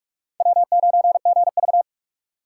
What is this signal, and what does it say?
Morse code, heard as a keyed tone, about 37 wpm, 700 Hz. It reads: W9GV
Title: Speed Racing - USA 1x2 Call Signs 30wpm